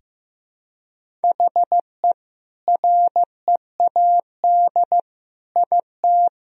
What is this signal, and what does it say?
Morse code, heard as a keyed tone, about 15 wpm, 700 Hz. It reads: HE READ IT
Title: Sentences from Top 1000 Words - Review Part 1 - 15wpm